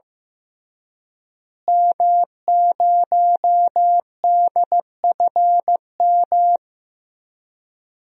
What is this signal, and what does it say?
Morse code, heard as a keyed tone, about 15 wpm, 700 Hz. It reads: M0DFM
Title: DXCC-100 Call Signs - 5 Characters - Repeated 3x - 15wpm